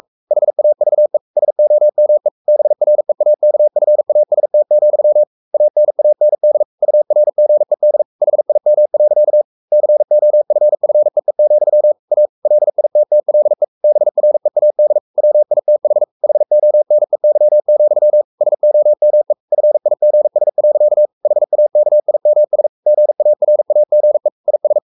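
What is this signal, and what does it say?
Morse code, heard as a keyed tone, about 44 wpm, 600 Hz. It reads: HAVE SOME BREAKFAST, ANAND URGED HIM. COFFEE, A LITTLE BREAD WITH HONEY, SOME FIGS. HAKIMS GARAGE IS
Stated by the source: Morse code at 43 wpm sending Ken Follett's novel Never in English, from CW med Gnister.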